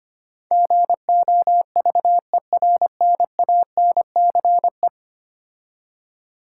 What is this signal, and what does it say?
Morse code, heard as a keyed tone, about 25 wpm, 700 Hz. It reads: GOVERNANCE
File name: 10 Letter Words - First 3 Letters Given - 25wpm